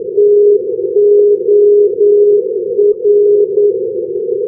Morse code, heard as a keyed tone, about 9 wpm, 410 Hz. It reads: TOR
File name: europe / fin / fin-299 ndb rovaniemi-toramo tor-416 np-s-haparanda-2012-k3-rb